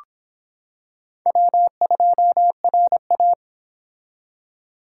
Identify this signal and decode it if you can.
Morse code, heard as a keyed tone, about 25 wpm, 700 Hz. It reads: W2RA